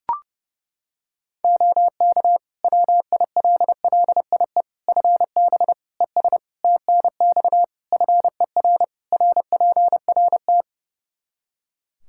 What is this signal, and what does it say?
Morse code, heard as a keyed tone, about 30 wpm, 700 Hz. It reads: OK WILLIE FB ES TNX FER RPRT
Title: Practice QSO - Exchange 2 - Info - 30wpm